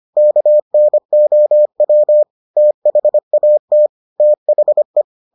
Morse code, heard as a keyed tone, about 25 wpm, 600 Hz. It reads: KNOW THAT THE